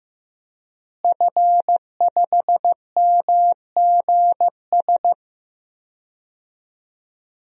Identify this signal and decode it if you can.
Morse code, heard as a keyed tone, about 15 wpm, 700 Hz. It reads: F5MGS
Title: DXCC-100 Call Signs - 5 Characters - Repeated 3x - 15wpm